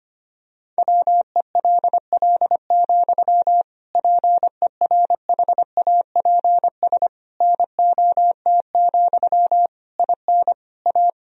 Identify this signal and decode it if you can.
Morse code, heard as a keyed tone, about 25 wpm, 700 Hz. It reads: WELL, PERHAPS NOT, IN A
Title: Sentences from Top 1000 Words - Review Part 2 - 25wpm